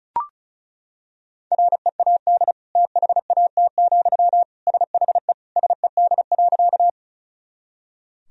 Morse code, heard as a keyed tone, about 34 wpm, 700 Hz. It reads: READ THAT, SHE SED.
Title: Sentences from Top 200 Words and Abbreviations 35wpm